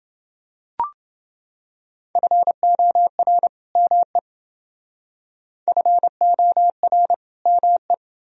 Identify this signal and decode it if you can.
Morse code, heard as a keyed tone, about 28 wpm, 700 Hz. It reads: FOR ME FOR ME